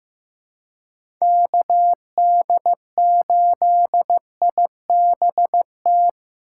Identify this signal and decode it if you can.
Morse code, heard as a keyed tone, about 15 wpm, 700 Hz. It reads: KD8IBT
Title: USA Call Signs - 6 Characters - 4x Character Spacing - 15wpm